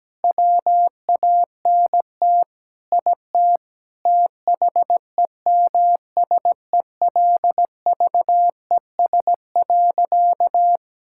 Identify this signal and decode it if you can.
Morse code, heard as a keyed tone, about 17 wpm, 700 Hz. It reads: WANT IT THEMSELVES.
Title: Sentences from Top 1000 Words - Review Part 5 - 17wpm